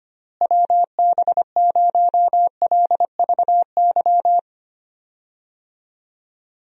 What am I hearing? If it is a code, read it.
Morse code, heard as a keyed tone, about 25 wpm, 700 Hz. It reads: WB0LVY
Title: USA Call Signs - 6 Characters - Repeated 2x - 25wpm